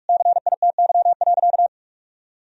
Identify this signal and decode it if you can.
Morse code, heard as a keyed tone, about 46 wpm, 700 Hz. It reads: CITY.
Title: Sentences from Top 1000 Words - Review Part 4 - 45wpm